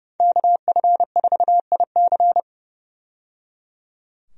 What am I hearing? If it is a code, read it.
Morse code, heard as a keyed tone, about 30 wpm, 700 Hz. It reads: KF4IC
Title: USA Call Signs - 5 Characters - 30wpm